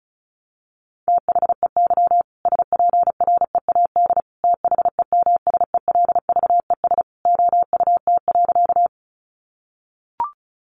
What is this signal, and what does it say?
Morse code, heard as a keyed tone, about 34 wpm, 700 Hz. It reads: THEY SPREAD THEMSELVES OUT.
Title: Sentences from Top 1200 Words 35wpm